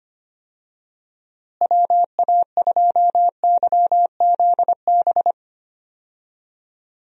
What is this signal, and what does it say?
Morse code, heard as a keyed tone, about 25 wpm, 700 Hz. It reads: WA2YZB
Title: USA Call Signs - 6 Characters - Repeated 3x - 25wpm